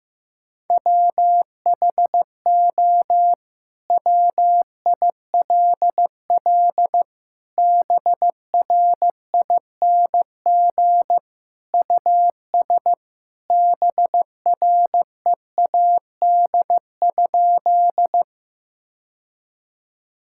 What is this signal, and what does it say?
Morse code, heard as a keyed tone, about 15 wpm, 700 Hz. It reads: WHO WILL BRING US BREAD?
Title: Sentences from Top 1600 Words 15wpm